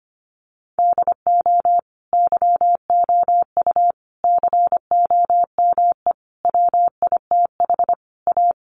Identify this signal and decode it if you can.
Morse code, heard as a keyed tone, about 25 wpm, 700 Hz. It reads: DO YOU COME WITH A
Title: Sentences from Top 1200 Words 25wpm